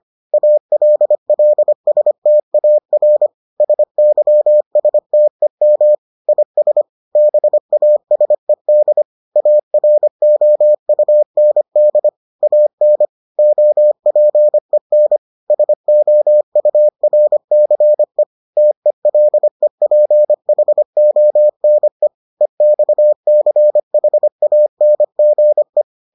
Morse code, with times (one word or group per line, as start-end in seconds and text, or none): 0.3-3.3 ALLSTAR
3.6-6.0 SYSTEM
6.3-6.8 IS
7.2-9.0 BASED
9.4-12.1 AROUND
12.4-13.1 AN
13.4-15.2 OPEN
15.5-18.2 SOURCE
18.6-22.1 TELEPHONE
22.4-25.8 EXCHANGE